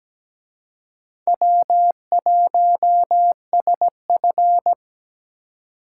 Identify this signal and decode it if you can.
Morse code, heard as a keyed tone, about 20 wpm, 700 Hz. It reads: W1SF
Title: Speed Racing - USA 1x2 Call Signs 17wpm